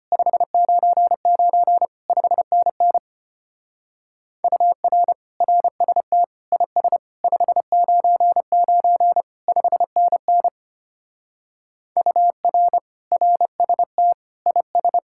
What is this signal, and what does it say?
Morse code, heard as a keyed tone, about 32 wpm, 700 Hz. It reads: 599 5NN UR RST IS 599 5NN UR RST IS